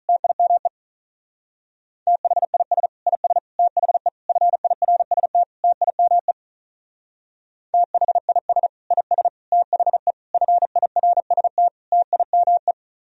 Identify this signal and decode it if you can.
Morse code, heard as a keyed tone, about 45 wpm, 700 Hz. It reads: TIME THIS IS THE FIRST TIME THIS IS THE FIRST TIME